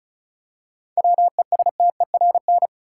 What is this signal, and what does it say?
Morse code, heard as a keyed tone, about 36 wpm, 700 Hz. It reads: WESTERN
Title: Seven Letter Words - 35wpm